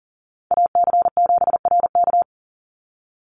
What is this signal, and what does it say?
Morse code, heard as a keyed tone, about 40 wpm, 700 Hz. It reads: AC7RK
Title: USA Call Signs 40wpm